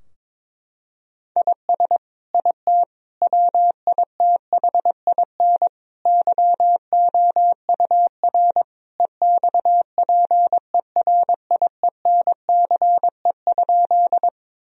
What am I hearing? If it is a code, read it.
Morse code, heard as a keyed tone, about 21 wpm, 700 Hz. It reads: IS IT WITHIN YOUR EXPERIENCE?
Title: Sentences from Top 1000 Words - Review Part 4 - 22wpm